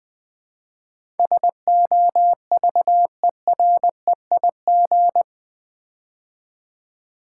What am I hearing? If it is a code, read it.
Morse code, heard as a keyed tone, about 20 wpm, 700 Hz. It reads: SOVEREIG
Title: Build A Word - 4 Letters At A Time - 11-Letter Words - 20wpm